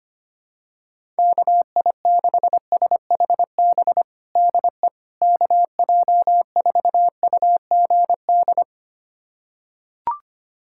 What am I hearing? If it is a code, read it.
Morse code, heard as a keyed tone, about 25 wpm, 700 Hz. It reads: KI6SHB DE KJ4UGD
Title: USA Callsign DE Callsign 25wpm